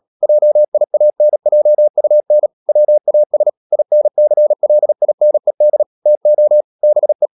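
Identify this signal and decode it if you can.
Morse code, heard as a keyed tone, about 37 wpm, 600 Hz. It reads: JIANJUN WAS INCLINED TO BE
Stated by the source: Morse code practice at 35–39 wpm sending English text of Ken Follett's novel Never, from CW med Gnister.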